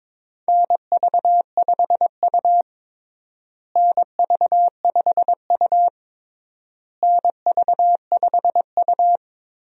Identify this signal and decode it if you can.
Morse code, heard as a keyed tone, about 22 wpm, 700 Hz. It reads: NV5U NV5U NV5U
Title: USA Call Signs - 4 Characters - Repeated 3x - 22wpm